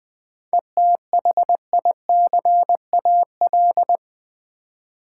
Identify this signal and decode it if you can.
Morse code, heard as a keyed tone, about 20 wpm, 690 Hz. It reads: ETHICAL
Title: 7 Letter Words - First 3 Letters Given - 20wpm